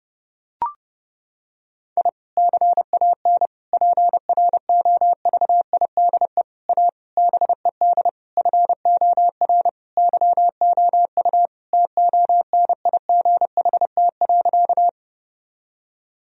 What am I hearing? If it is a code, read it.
Morse code, heard as a keyed tone, about 30 wpm, 700 Hz. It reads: I CAN PROVIDE A BED FOR YOU TONIGHT.
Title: Sentences from Top 1000 Words - Review Part 3 - 30wpm